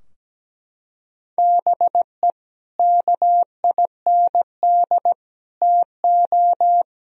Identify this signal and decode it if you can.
Morse code, heard as a keyed tone, about 17 wpm, 700 Hz. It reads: BE KIND TO
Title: Sentences from Top 1000 Words - Review Part 2 - 17wpm